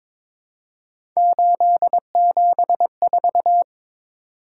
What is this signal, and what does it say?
Morse code, heard as a keyed tone, about 22 wpm, 700 Hz. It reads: 874
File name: practice / 22wpm / Three Letter-Number Combinations - 22wpm